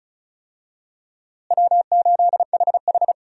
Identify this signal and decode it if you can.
Morse code, heard as a keyed tone, about 36 wpm, 700 Hz. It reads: W8HH
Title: USA Call Signs - 4 Characters - Repeated 3x - 35wpm